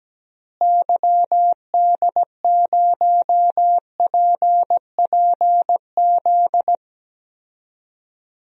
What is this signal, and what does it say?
Morse code, heard as a keyed tone, about 17 wpm, 700 Hz. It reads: YD0PPZ
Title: DXCC-100 Call Signs - 6 Characters - Repeated 3x - 17wpm